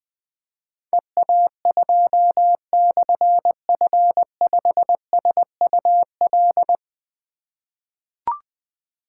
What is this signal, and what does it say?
Morse code, heard as a keyed tone, about 20 wpm, 700 Hz. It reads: EA2/F5SUL
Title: Difficult Call Signs 20wpm